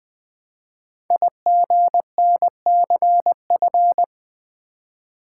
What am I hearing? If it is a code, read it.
Morse code, heard as a keyed tone, about 20 wpm, 700 Hz. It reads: IGNCF
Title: Code Groups - 5 Letters - 20wpm